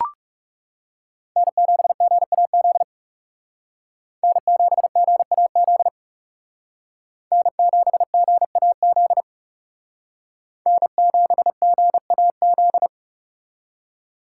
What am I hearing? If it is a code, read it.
Morse code, heard as a keyed tone, about 38 wpm, 700 Hz. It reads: N7GAZ N7GAZ N7GAZ N7GAZ